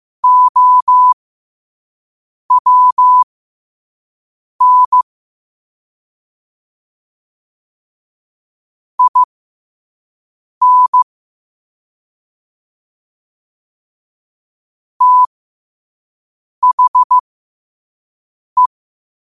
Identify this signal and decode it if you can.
Morse code, heard as a keyed tone, about 15 wpm, spaced as for 5 wpm, 1000 Hz. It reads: OWN IN THE
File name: qotdfiles / 5wpm